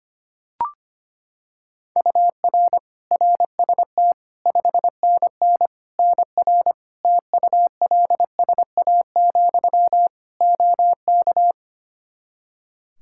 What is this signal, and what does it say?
Morse code, heard as a keyed tone, about 25 wpm, 700 Hz. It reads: UR RST 5NN NR TULSA, OK
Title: Practice QSO - Exchange 1 - Beginning - 25wpm